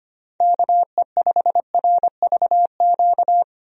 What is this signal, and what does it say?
Morse code, heard as a keyed tone, about 25 wpm, 700 Hz. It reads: KE5RVQ
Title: USA Call Signs - 6 Characters - 25wpm